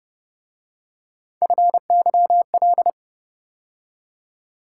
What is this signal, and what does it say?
Morse code, heard as a keyed tone, about 31 wpm, 700 Hz. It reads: FYL